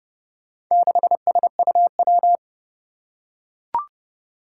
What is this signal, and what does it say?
Morse code, heard as a keyed tone, about 30 wpm, 700 Hz. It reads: 6SUW
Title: Four Challenging Letter-Number Combinations 30wpm